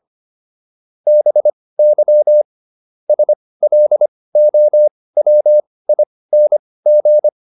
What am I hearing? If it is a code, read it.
Morse code, heard as a keyed tone, about 25 wpm, spaced as for 18 wpm, 600 Hz. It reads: BY SLOWING